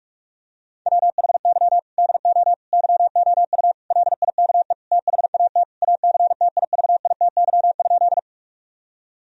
Morse code, heard as a keyed tone, about 46 wpm, 700 Hz. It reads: WHY DO YOU LIKE THAT ACTIVITY?
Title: Sentences from Top 1000 Words - Review Part 4 - 45wpm